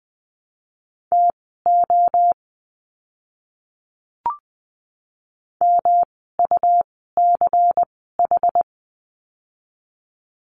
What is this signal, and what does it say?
Morse code, heard as a keyed tone, about 20 wpm, spaced as for 14 wpm, 700 Hz. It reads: TO MUCH
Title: 206 Lesson 33f - Letter j - Words 20wpm